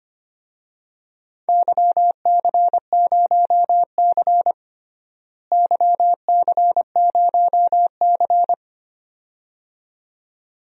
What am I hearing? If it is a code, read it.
Morse code, heard as a keyed tone, about 25 wpm, 700 Hz. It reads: YC0C YC0C
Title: DXCC-100 Call Signs - 4 Characters - Repeated 2x - 25wpm